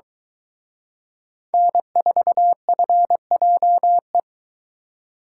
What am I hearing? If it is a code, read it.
Morse code, heard as a keyed tone, about 22 wpm, 700 Hz. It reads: N4FJE